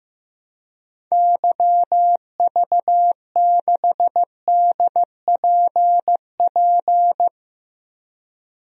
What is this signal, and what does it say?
Morse code, heard as a keyed tone, about 15 wpm, 700 Hz. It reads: YV6DPP
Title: DXCC-100 Call Signs - 6 Characters - Repeated 2x - 15wpm